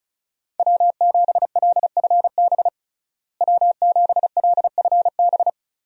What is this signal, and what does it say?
Morse code, heard as a keyed tone, about 36 wpm, 700 Hz. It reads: W7LFB W7LFB
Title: USA Call Signs - 5 Characters - Repeated 3x - 35wpm